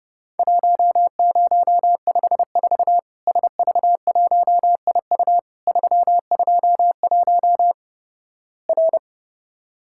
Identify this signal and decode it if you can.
Morse code, heard as a keyed tone, about 32 wpm, 700 Hz. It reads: 1054 SV1IU 321 R
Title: Worked All Europe DX Contest - QTC Exchange - 30wpm